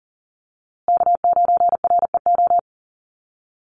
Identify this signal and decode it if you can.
Morse code, heard as a keyed tone, about 40 wpm, 700 Hz. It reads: K9REO
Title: USA Call Signs Repeated 3x 40wpm